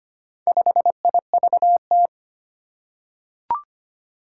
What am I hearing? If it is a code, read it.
Morse code, heard as a keyed tone, about 25 wpm, 700 Hz. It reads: 5IVT
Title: Four Challenging Letter-Number Combinations 25wpm